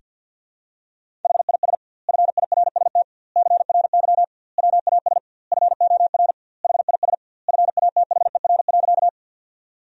Morse code, heard as a keyed tone, about 51 wpm, 700 Hz. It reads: HIS FIRST CRY WAS FOR HIS FATHER.
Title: 2x Word-Spacing - Sentences from Top 500 Words - 50wpm